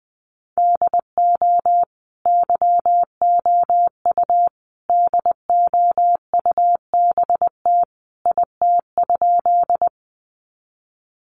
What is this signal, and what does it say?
Morse code, heard as a keyed tone, about 20 wpm, 700 Hz. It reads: DO YOU DOUBT IT?
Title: Sentences from Top 1500 Words 20wpm